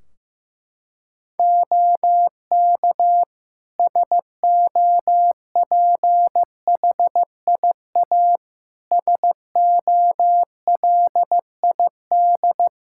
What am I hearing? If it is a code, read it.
Morse code, heard as a keyed tone, about 16 wpm, 700 Hz. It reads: OK SOPHIA SOLID
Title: Practice QSO - Exchange 3 - More Info - 15wpm